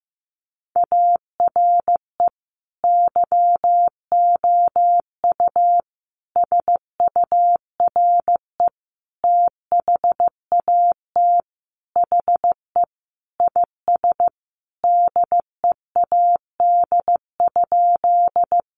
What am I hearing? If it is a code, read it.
Morse code, heard as a keyed tone, about 15 wpm, 700 Hz. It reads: ARE YOU SURE THAT HE IS DEAD?
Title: Sentences from Top 1500 Words 15wpm